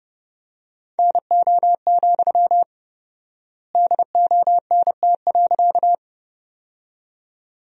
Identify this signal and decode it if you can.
Morse code, heard as a keyed tone, about 30 wpm, 700 Hz. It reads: NO, DONT.
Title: 4x Word-Spacing - Sentences from Top 300 Words - 30wpm